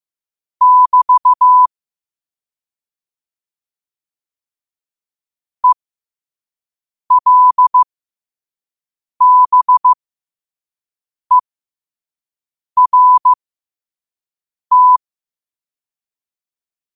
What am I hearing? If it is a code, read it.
Morse code, heard as a keyed tone, about 15 wpm, spaced as for 5 wpm, 1000 Hz. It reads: <BT> ELBERT
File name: 5wpm